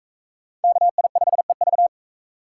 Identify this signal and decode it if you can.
Morse code, heard as a keyed tone, about 41 wpm, 700 Hz. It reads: KI5EV